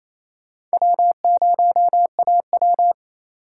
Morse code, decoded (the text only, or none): W0AW